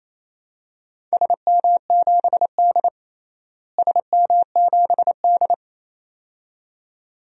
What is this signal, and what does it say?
Morse code, heard as a keyed tone, about 28 wpm, 700 Hz. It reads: SM7D SM7D
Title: DXCC-100 Call Signs - 4 Characters - Repeated 2x - 28wpm